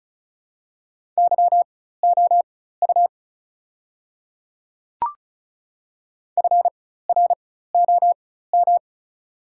Morse code, heard as a keyed tone, about 34 wpm, spaced as for 16 wpm, 700 Hz. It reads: YOU FROM
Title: 4x Character Spacing - Top 100 Words - 35wpm